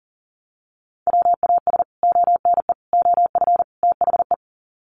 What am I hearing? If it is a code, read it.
Morse code, heard as a keyed tone, about 40 wpm, 700 Hz. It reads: WAS ONE OF THE
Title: Sets of 4 Words 40wpm